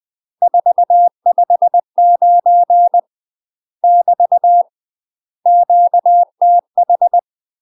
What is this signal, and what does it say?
Morse code, heard as a keyed tone, about 20 wpm, 700 Hz. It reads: 459 <BT> QTH
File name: UK MASTER QSOs TRAINING / 20WPM / UK MASTER QSOs 0043